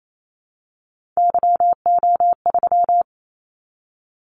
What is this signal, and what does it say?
Morse code, heard as a keyed tone, about 28 wpm, 700 Hz. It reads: YO3
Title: Callsign Prefixes for Top 25 DXCC - 28wpm